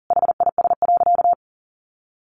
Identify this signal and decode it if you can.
Morse code, heard as a keyed tone, about 40 wpm, 700 Hz. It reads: HIS.
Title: Sentences from Top 900 Words 40wpm